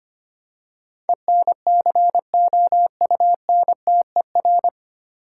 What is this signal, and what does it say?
Morse code, heard as a keyed tone, about 25 wpm, 700 Hz. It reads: ENCOUNTER
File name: practice / 25wpm / 9 Letter Words - First 3 Letters Given - 25wpm